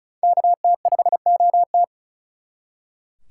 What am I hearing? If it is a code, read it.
Morse code, heard as a keyed tone, about 43 wpm, 700 Hz. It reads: KT5OT
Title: Speed Racing - USA 2x2 Call Signs 35wpm